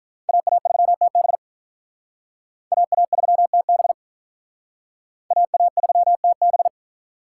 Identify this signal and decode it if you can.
Morse code, heard as a keyed tone, about 45 wpm, 700 Hz. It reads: AA3TB AA3TB AA3TB